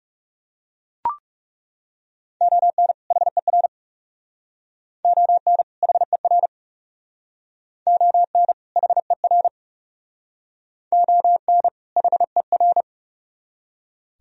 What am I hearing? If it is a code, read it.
Morse code, heard as a keyed tone, about 38 wpm, 700 Hz. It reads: ON HER ON HER ON HER ON HER
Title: Speed-Racing - Sets of 2 Words 30wpm